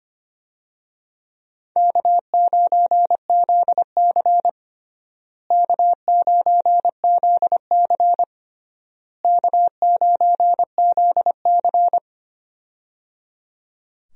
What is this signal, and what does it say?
Morse code, heard as a keyed tone, about 25 wpm, 700 Hz. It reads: K9ZC K9ZC K9ZC
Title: USA Call Signs - 4 Characters - Repeated 3x - 25wpm